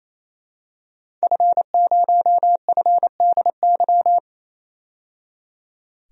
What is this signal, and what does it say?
Morse code, heard as a keyed tone, about 28 wpm, 700 Hz. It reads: F0FDY
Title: DXCC-100 Call Signs - 5 Characters - 28wpm